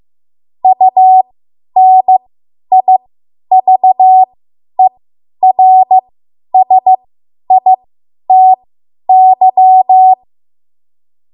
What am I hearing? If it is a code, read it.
Morse code, heard as a keyed tone, about 15 wpm, spaced as for 10 wpm, 750 Hz. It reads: UNIVERSITY
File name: Morse / 10WPM